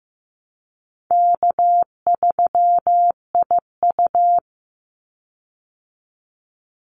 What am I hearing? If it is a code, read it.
Morse code, heard as a keyed tone, about 15 wpm, 700 Hz. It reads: K3IU
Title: Easy Call Signs Repeated 3x - 15wpm